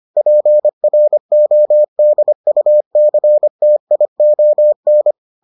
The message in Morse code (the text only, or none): PRODUCTION